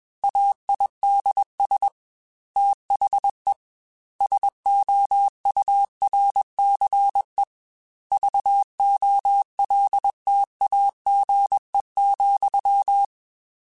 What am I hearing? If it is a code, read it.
Morse code, heard as a keyed tone, about 21 wpm, 800 Hz. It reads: AIDS THE SOURCE VOLTAGE,